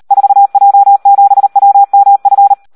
Morse code, heard as a keyed tone, about 38 wpm, 800 Hz. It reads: 4J7WMF
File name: Audio practica CW / SD 1 38